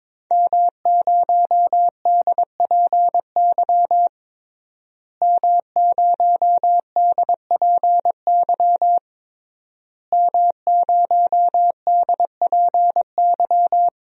Morse code, heard as a keyed tone, about 22 wpm, 700 Hz. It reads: M0DPY M0DPY M0DPY
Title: DXCC-100 Call Signs - 5 Characters - Repeated 3x - 22wpm